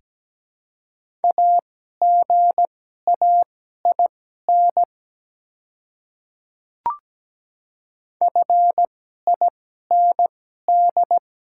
Morse code, heard as a keyed tone, about 17 wpm, spaced as for 12 wpm, 700 Hz. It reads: AGAIN FIND
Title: 2x Character Spacing - Top 200 Words - 17wpm